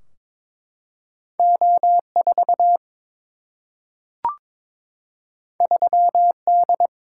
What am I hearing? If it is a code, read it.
Morse code, heard as a keyed tone, about 21 wpm, 700 Hz. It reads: O4 3D
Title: Single-Letter Single-Number 22wpm